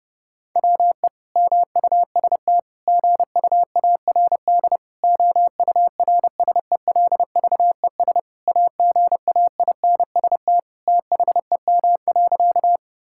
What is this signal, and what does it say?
Morse code, heard as a keyed tone, about 30 wpm, 700 Hz. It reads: WE MUST GUARD OURSELVES AGAINST THEM.
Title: Sentences from Top 1600 Words 30wpm